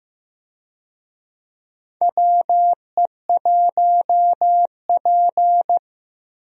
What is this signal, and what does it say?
Morse code, heard as a keyed tone, about 15 wpm, 700 Hz. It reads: WE1P
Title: USA Call Signs - 4 Characters - Repeated 3x - 15wpm